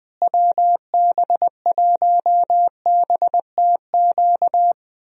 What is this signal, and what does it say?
Morse code, heard as a keyed tone, about 20 wpm, 700 Hz. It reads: WB1BTQ
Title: USA Call Signs - 6 Characters - Repeated 3x - 20wpm